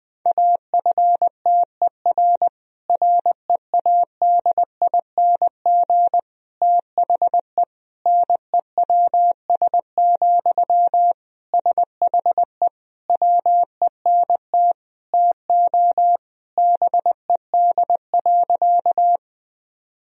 Morse code, heard as a keyed tone, about 20 wpm, 700 Hz. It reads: AFTER READING THE NEWS, SHE WENT TO BED.
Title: Sentences from Top 1000 Words - Review Part 3 - 20wpm